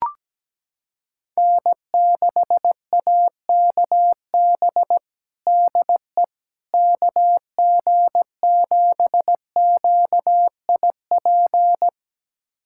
Morse code, heard as a keyed tone, about 17 wpm, 700 Hz. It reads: N6AKB DE KG7QIP
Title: USA Callsign DE Callsign 17wpm